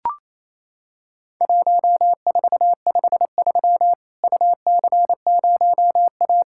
Morse code, heard as a keyed tone, about 27 wpm, 700 Hz. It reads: 1453 UC0A
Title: Worked All Europe DX Contest - QTC Exchange - 28wpm